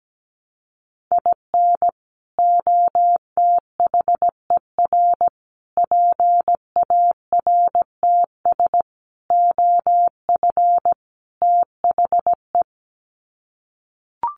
Morse code, heard as a keyed tone, about 17 wpm, 700 Hz. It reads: IN OTHER PARTS OF THE
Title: Sets of 5 Words 17wpm